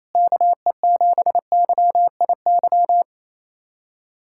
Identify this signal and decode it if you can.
Morse code, heard as a keyed tone, about 28 wpm, 700 Hz. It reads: KE7YIY
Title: USA Callsign DE Callsign - 6 Characters - 28wpm